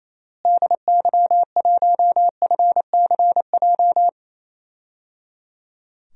Morse code, heard as a keyed tone, about 28 wpm, 700 Hz. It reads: DY1FCJ
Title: DXCC-100 Call Signs - 6 Characters - Repeated 3x - 28wpm